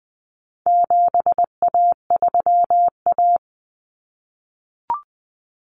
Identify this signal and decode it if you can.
Morse code, heard as a keyed tone, about 20 wpm, 700 Hz. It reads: 7A3A
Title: Common Cut Numbers - 4-Digit 20wpm